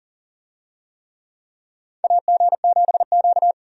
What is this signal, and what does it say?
Morse code, heard as a keyed tone, about 40 wpm, 700 Hz. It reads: AG7Q
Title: USA Call Signs - 4 Characters - 40wpm